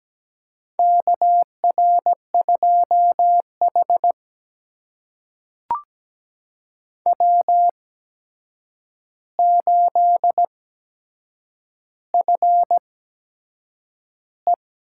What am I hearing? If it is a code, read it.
Morse code, heard as a keyed tone, about 17 wpm, spaced as for 7 wpm, 700 Hz. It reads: KR2H W8FE
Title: USA Call Signs - 4 Characters - 8x Character Spacing - 17wpm